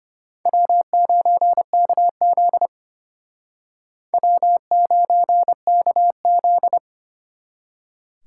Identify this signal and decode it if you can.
Morse code, heard as a keyed tone, about 29 wpm, 700 Hz. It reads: W9KZ W9KZ
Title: Speed Racing - USA Call Signs 25wpm